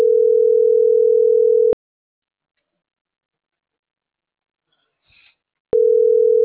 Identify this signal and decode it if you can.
Ringback tone (North America), 440 + 480 Hz.